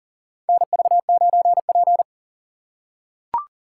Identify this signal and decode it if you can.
Morse code, heard as a keyed tone, about 38 wpm, 700 Hz. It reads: NV9P
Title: USA Call Signs - 2x Character Spacing - 40wpm